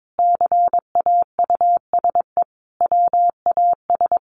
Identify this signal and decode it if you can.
Morse code, heard as a keyed tone, about 22 wpm, 700 Hz. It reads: CAUSE WAS
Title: Sentences from Top 1500 Words 22wpm